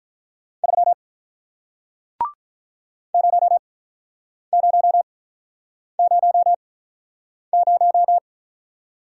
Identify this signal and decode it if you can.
Morse code, heard as a keyed tone, about 46 wpm, spaced as for 9 wpm, 700 Hz. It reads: 3 0000